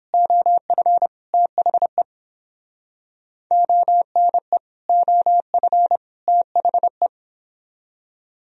Morse code, heard as a keyed tone, about 29 wpm, 700 Hz. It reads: OF THE ONE OF THE